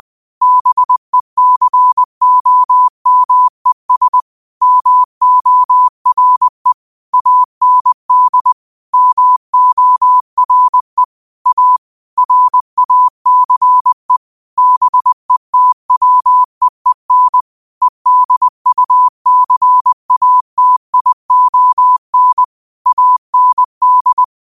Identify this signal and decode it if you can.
Morse code, heard as a keyed tone, about 20 wpm, 1000 Hz. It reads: BECOMES MORE AND MORE A RACE BETWEEN EDUCATION AND
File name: qotdfiles / 20wpm